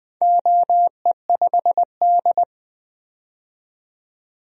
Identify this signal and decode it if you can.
Morse code, heard as a keyed tone, about 20 wpm, 700 Hz. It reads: OE5D
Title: DXCC-100 Call Signs - 4 Characters - 20wpm